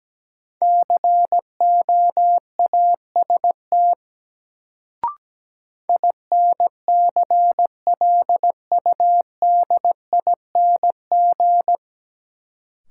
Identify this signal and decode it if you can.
Morse code, heard as a keyed tone, about 17 wpm, 700 Hz. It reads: COAST INCLUDING
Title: Sentences from Top 1000 Words - Review Part 4 - New Word Drill - 17wpm